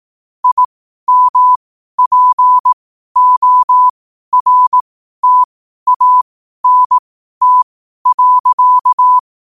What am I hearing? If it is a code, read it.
Morse code, heard as a keyed tone, about 18 wpm, spaced as for 12 wpm, 1000 Hz. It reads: IMPORTANT.
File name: qotdfiles / 12wpm